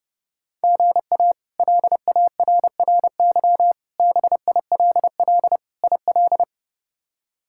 Morse code, heard as a keyed tone, about 32 wpm, 700 Hz. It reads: GA LARRY BILL IL
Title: K1USN SST QSO Practice - 30wpm